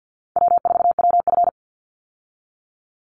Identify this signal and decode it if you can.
Morse code, heard as a keyed tone, about 49 wpm, 700 Hz. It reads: W4WF
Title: Easy Call Signs Repeated 2x - 50wpm